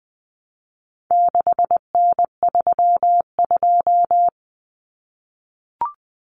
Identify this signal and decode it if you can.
Morse code, heard as a keyed tone, about 20 wpm, 700 Hz. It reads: 6N32
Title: Common Cut Numbers - 4-Digit 20wpm